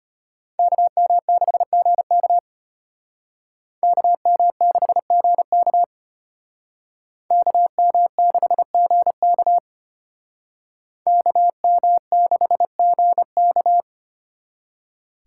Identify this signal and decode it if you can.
Morse code, heard as a keyed tone, about 33 wpm, 700 Hz. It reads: KM6GK KM6GK KM6GK KM6GK